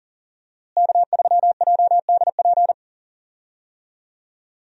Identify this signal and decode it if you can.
Morse code, heard as a keyed tone, about 37 wpm, 700 Hz. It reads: K3JDP
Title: USA Call Signs - 5 Characters - Repeated 2x - 40wpm